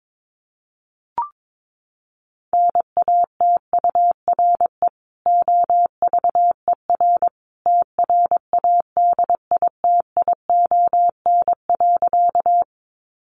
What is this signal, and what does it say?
Morse code, heard as a keyed tone, about 22 wpm, 700 Hz. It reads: NATURE OVER TRADITION.
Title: Sentences from Top 1500 Words 22wpm